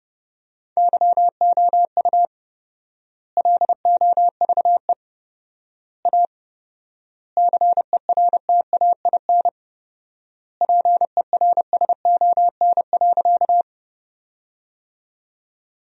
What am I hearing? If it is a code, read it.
Morse code, heard as a keyed tone, about 32 wpm, 700 Hz. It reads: YOU LOVE A CERTAIN PERSON.
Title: 4x Word-Spacing - Sentences from Top 500 Words - 30wpm